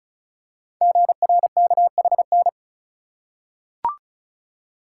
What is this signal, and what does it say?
Morse code, heard as a keyed tone, about 36 wpm, 690 Hz. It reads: GRKHN
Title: Code Groups - 5 Letter:Numbers - 35wpm